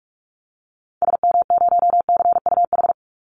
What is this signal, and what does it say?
Morse code, heard as a keyed tone, about 44 wpm, 700 Hz. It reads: SM0CUH
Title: Call Signs Repeated 2x - 45wpm